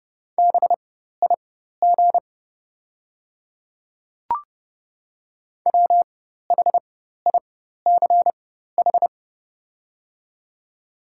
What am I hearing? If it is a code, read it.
Morse code, heard as a keyed tone, about 30 wpm, spaced as for 14 wpm, 700 Hz. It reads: BIG WHICH
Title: 4x Character Spacing - Top 200 Words - 30wpm